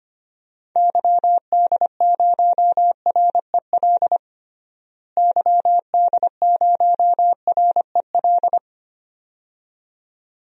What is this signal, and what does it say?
Morse code, heard as a keyed tone, about 25 wpm, 700 Hz. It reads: YD0REL YD0REL
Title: DXCC-100 Call Signs - 6 Characters - Repeated 2x - 25wpm